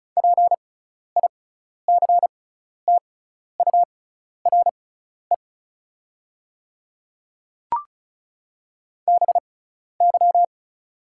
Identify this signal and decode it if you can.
Morse code, heard as a keyed tone, about 34 wpm, spaced as for 12 wpm, 690 Hz. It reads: PICTURE BY
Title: 6x Character Spacing - Top 200 Words - 35wpm